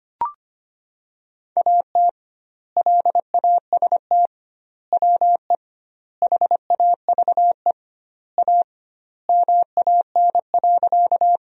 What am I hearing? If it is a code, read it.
Morse code, heard as a keyed tone, about 25 wpm, 700 Hz. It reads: AT LAST WE HAVE A MAN.
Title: 2x Word-Spacing - Sentences from Top 300 Words - 25wpm